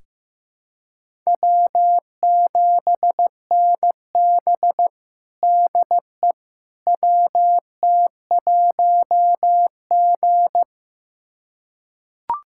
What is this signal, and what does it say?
Morse code, heard as a keyed tone, about 15 wpm, 700 Hz. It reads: W7NB DE WT1G
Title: USA Callsign DE Callsign - 4 Characters - 15wpm